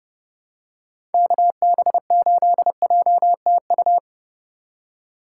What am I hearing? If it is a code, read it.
Morse code, heard as a keyed tone, about 29 wpm, 700 Hz. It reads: KB8JTU